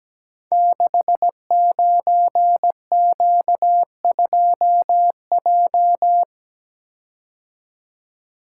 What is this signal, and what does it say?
Morse code, heard as a keyed tone, about 17 wpm, 700 Hz. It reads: 69Q2J
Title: Code Groups - 5 Letter:Numbers - 17wpm